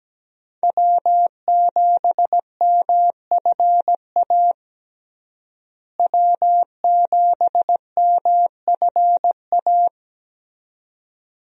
Morse code, heard as a keyed tone, about 17 wpm, 700 Hz. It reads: W7MFA W7MFA
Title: USA Call Signs - 5 Characters - Repeated 2x - 17wpm